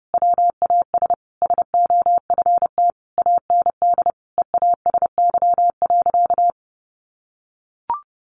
Morse code, heard as a keyed tone, about 30 wpm, 700 Hz. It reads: WAS SOFT AND EASY.
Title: Sentences from Top 1500 Words 30wpm